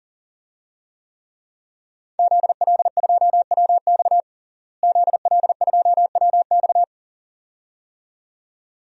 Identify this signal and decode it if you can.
Morse code, heard as a keyed tone, about 40 wpm, 700 Hz. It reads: ZL2WX ZL2WX
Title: DXCC-100 Call Signs - 5 Characters - Repeated 2x - 40wpm